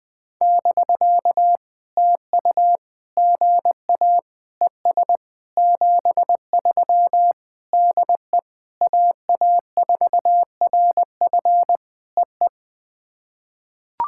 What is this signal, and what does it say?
Morse code, heard as a keyed tone, about 20 wpm, 700 Hz. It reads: <BK> TU GA ES 73 DE AA4RF EE
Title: POTA QSO Practice - 20wpm